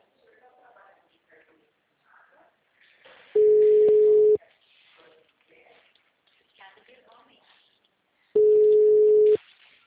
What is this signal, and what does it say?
Ringback tone (Europe, 425 Hz), with speech.
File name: out- -2001- - - .368